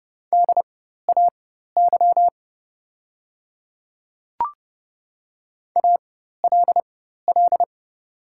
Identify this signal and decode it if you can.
Morse code, heard as a keyed tone, about 30 wpm, spaced as for 14 wpm, 700 Hz. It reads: DAY ALL
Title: 4x Character Spacing - Top 200 Words - 30wpm